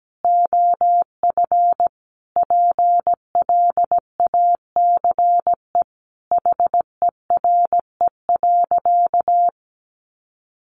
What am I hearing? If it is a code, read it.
Morse code, heard as a keyed tone, about 17 wpm, 700 Hz. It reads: OF PLACE HERE.
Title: Sentences from Top 1500 Words 17wpm